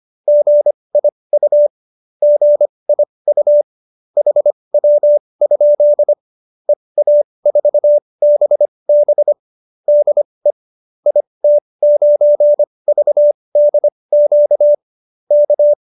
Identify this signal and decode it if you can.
Morse code, heard as a keyed tone, about 25 wpm, spaced as for 20 wpm, 590 Hz. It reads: GIU GIU HW? EA4BB DE IT9VDQ K
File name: qso215b 20